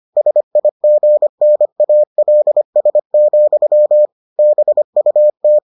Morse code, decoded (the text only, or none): SIGNALS, BUT